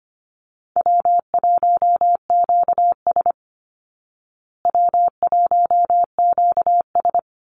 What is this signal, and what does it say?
Morse code, heard as a keyed tone, about 25 wpm, 700 Hz. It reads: W1QS W1QS
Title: Easy Call Signs Repeated 3x - 25wpm